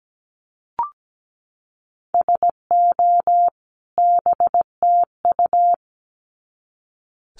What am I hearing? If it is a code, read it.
Morse code, heard as a keyed tone, about 17 wpm, 700 Hz. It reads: SO BTU
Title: Common QSO Phrases 17wpm